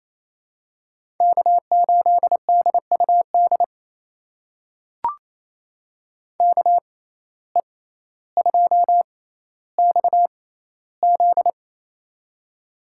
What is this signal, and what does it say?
Morse code, heard as a keyed tone, about 28 wpm, spaced as for 14 wpm, 700 Hz. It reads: K8DUD KE2XZ
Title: USA Call Signs - 5 Characters - 6x Character Spacing - 28wpm